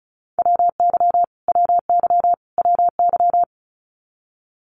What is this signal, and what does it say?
Morse code, heard as a keyed tone, about 34 wpm, 700 Hz. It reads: WY WY WY
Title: US States Repeated 3x 35wpm